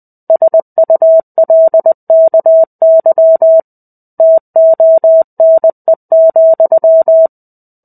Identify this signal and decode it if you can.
Morse code, heard as a keyed tone, about 20 wpm, 650 Hz. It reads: SULKY TONE,
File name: 05-Chapter-05 20 WPM